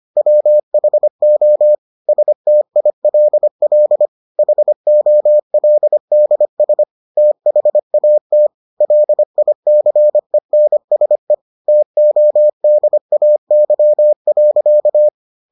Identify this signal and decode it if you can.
Morse code, heard as a keyed tone, about 25 wpm, 600 Hz. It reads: WHO STILL HOLDS THAT LICENSE TODAY.